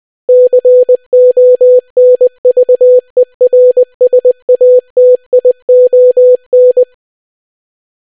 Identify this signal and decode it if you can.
Morse code, heard as a keyed tone, about 20 wpm, 500 Hz. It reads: CONVERSATION